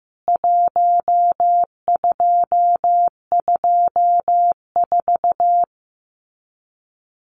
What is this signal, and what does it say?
Morse code, heard as a keyed tone, about 15 wpm, 700 Hz. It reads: 1224
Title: CWT Exchange 15wpm